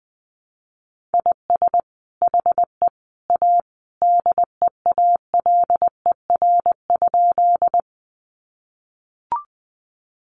Morse code, heard as a keyed tone, about 20 wpm, 700 Hz. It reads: IS HE A DEALER?
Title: Sentences from Top 1100 Words 20wpm